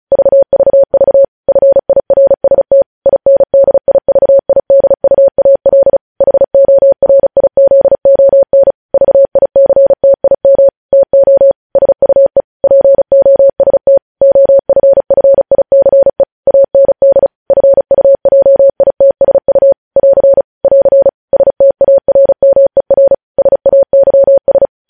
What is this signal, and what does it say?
Morse code, heard as a keyed tone, about 36 wpm, 550 Hz. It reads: VVV FIRST INDIVIDUAL HORIZON VICTIM TO SUE POST OFFICE AND FUJITSU <AR> <AR> STARMER SAYS
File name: podcast / News headlines-1x-35x35